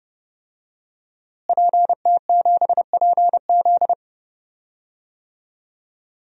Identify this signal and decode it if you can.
Morse code, heard as a keyed tone, about 30 wpm, 700 Hz. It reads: PT7PZ
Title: DXCC-100 Call Signs - 5 Characters - Repeated 2x - 30wpm